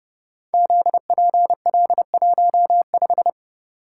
Morse code, heard as a keyed tone, about 28 wpm, 700 Hz. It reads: ZPL15
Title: Code Groups - 5 Letter:Numbers - 30wpm